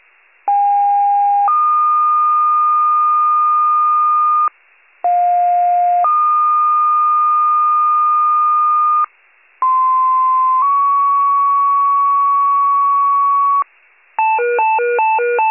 Two-tone page, heard as a one-second tone, then a three-second tone, about 810 then 1220 Hz, about 710 then 1220 Hz, about 1040 then 1120 Hz. It is heard from 0.5 to 13.6 s.